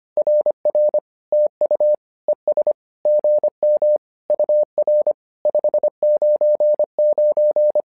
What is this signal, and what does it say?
Morse code, heard as a keyed tone, about 25 wpm, 600 Hz. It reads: RR TU ES GM UR 599
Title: SOTA QSO Practice 25wpm